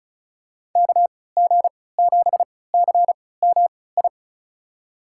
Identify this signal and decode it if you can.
Morse code, heard as a keyed tone, about 36 wpm, spaced as for 20 wpm, 700 Hz. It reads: KG7CMI
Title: USA Call Signs - 6 Characters - 3x Character Spacing - 35wpm